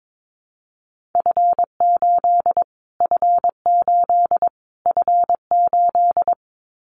Morse code, heard as a keyed tone, about 22 wpm, 700 Hz. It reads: F8 F8 F8
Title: Callsign Prefixes for Top 25 DXCC - Repeated 3x - 22wpm